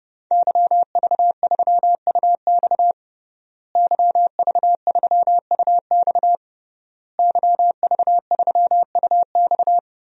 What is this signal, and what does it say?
Morse code, heard as a keyed tone, about 31 wpm, 700 Hz. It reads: YV3UX YV3UX YV3UX
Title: DXCC-100 Call Signs - 5 Characters - Repeated 3x - 30wpm